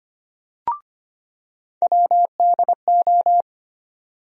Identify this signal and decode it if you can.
Morse code, heard as a keyed tone, about 25 wpm, 700 Hz. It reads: WDO
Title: USA Call Sign Postfixes - 3 Letters 25wpm